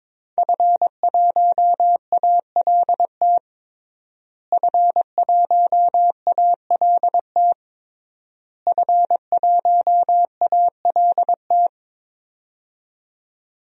Morse code, heard as a keyed tone, about 22 wpm, 700 Hz. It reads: F1ALT F1ALT F1ALT
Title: DXCC-100 Call Signs - 5 Characters - Repeated 3x - 22wpm